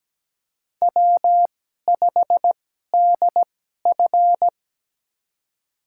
Morse code, heard as a keyed tone, about 17 wpm, spaced as for 12 wpm, 700 Hz. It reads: W5DF
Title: USA Call Signs - 4 Characters - 2x Character Spacing - 17wpm